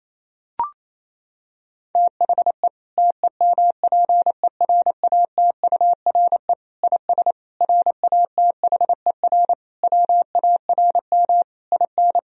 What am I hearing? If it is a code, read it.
Morse code, heard as a keyed tone, about 28 wpm, 700 Hz. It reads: THE TEMPERATURE IS RATHER WARM IN
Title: Sentences from Top 1000 Words - Review Part 4 - 28wpm